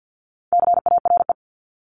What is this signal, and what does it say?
Morse code, heard as a keyed tone, about 50 wpm, 700 Hz. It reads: CARE
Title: Top 500 Words 50wpm